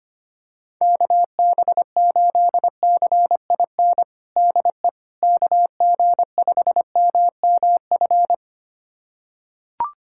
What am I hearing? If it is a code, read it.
Morse code, heard as a keyed tone, about 25 wpm, 700 Hz. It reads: KB8CIN DE KG5MMF
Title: USA Callsign DE Callsign - 6 Characters - 25wpm